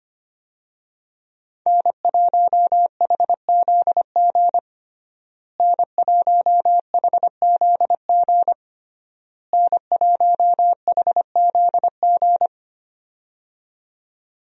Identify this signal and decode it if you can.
Morse code, heard as a keyed tone, about 25 wpm, 700 Hz. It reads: N1HZG N1HZG N1HZG
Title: USA Call Signs - 5 Characters - Repeated 3x - 25wpm